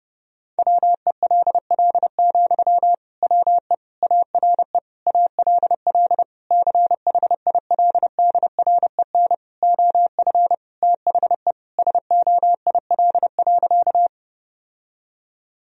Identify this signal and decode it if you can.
Morse code, heard as a keyed tone, about 30 wpm, 700 Hz. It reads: WELL, WE ARE ALL CHILDREN OF THE SOIL.
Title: Sentences from Top 1000 Words - Review Part 4 - 30wpm